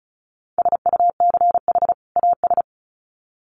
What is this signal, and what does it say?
Morse code, heard as a keyed tone, about 34 wpm, 700 Hz. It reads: SUCH AS
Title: Sets of 2 Words 35wpm